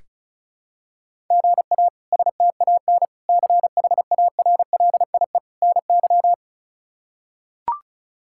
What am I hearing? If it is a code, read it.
Morse code, heard as a keyed tone, about 36 wpm, 700 Hz. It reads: GA STAN CHARLIE NY
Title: K1USN SST QSO Practice - 35wpm